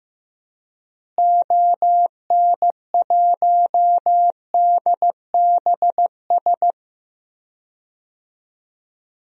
Morse code, heard as a keyed tone, about 15 wpm, 700 Hz. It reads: ON1DBS
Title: DXCC-100 Call Signs - 6 Characters - Repeated 2x - 15wpm